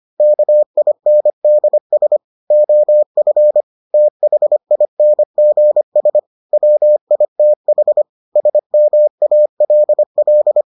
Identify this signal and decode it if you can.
Morse code, heard as a keyed tone, about 25 wpm, 600 Hz. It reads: KINDS OF THINGS WITH SMALL